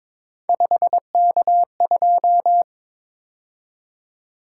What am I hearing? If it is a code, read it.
Morse code, heard as a keyed tone, about 22 wpm, 700 Hz. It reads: 5K2
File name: practice / 22wpm / Three Letter-Number Combinations - 22wpm